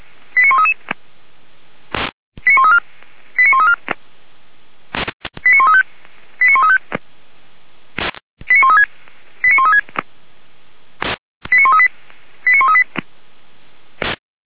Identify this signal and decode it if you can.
Five-tone selective call (ZVEI), heard as quick runs of five tones, tones of about 70 ms. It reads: 89133 89134 89134 89135 89135 89136 89136 89138 89138